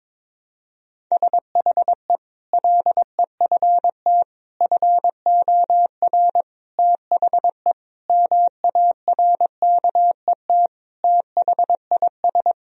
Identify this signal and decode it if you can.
Morse code, heard as a keyed tone, about 22 wpm, 700 Hz. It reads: SHE LEFT FOR THE MARKET THIS
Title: Sentences from Top 1000 Words - Review Part 3 - 22wpm